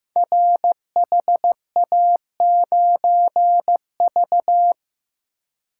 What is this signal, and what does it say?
Morse code, heard as a keyed tone, about 15 wpm, 700 Hz. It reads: RHA9V
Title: Code Groups - 5 Letter:Numbers - 15wpm